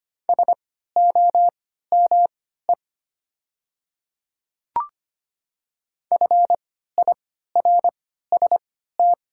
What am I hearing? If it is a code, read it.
Morse code, heard as a keyed tone, about 25 wpm, spaced as for 14 wpm, 700 Hz. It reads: SOME FIRST
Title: 3x Character Spacing - Top 100 Words - 25wpm